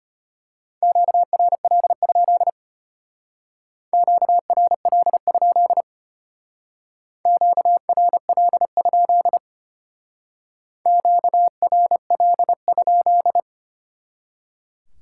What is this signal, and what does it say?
Morse code, heard as a keyed tone, about 32 wpm, 700 Hz. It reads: QRL? QRL? QRL? QRL?